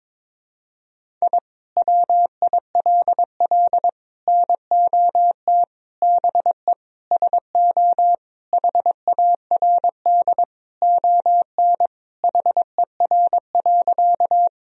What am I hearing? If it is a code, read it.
Morse code, heard as a keyed tone, about 21 wpm, 700 Hz. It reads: I WILL NOT BE SO HARD ON HER.
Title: Sentences from Top 1000 Words - Review Part 1 - 22wpm